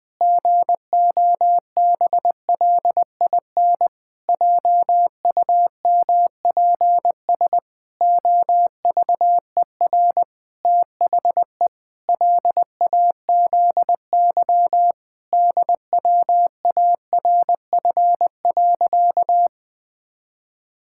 Morse code, heard as a keyed tone, about 21 wpm, 700 Hz. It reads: GOBLIN JUMPS OVER THE LAZY DWARF.
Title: Pangram Challenge v2 - 20wpm